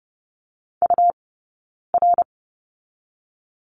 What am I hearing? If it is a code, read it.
Morse code, heard as a keyed tone, about 30 wpm, 700 Hz. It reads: U R
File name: Lesson 17d - Letter u - Two Characters 30wpm